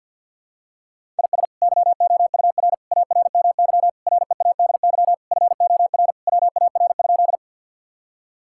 Morse code, heard as a keyed tone, about 49 wpm, 700 Hz. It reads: IS YOUR ARMY READY FOR WAR?
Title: Sentences from Top 1000 Words - Review Part 5 - 50wpm